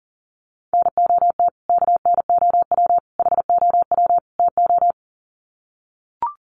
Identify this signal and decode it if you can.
Morse code, heard as a keyed tone, about 40 wpm, 700 Hz. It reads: NOT KNOW HOW TO
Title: Sets of 4 Words 40wpm